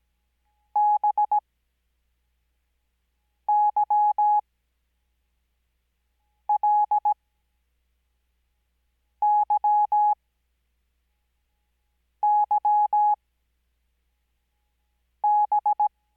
Morse code, heard as a keyed tone, about 17 wpm, spaced as for 4 wpm, 840 Hz. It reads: BYLYYB
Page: Farnsworth Spacing: 15WPM Character speed Spaced at 5WPM.